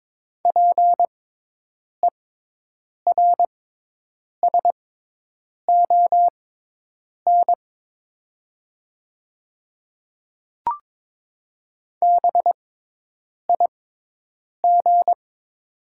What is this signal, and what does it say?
Morse code, heard as a keyed tone, about 22 wpm, spaced as for 8 wpm, 700 Hz. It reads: PERSON BIG
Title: 6x Character Spacing - Top 500 Words - 22wpm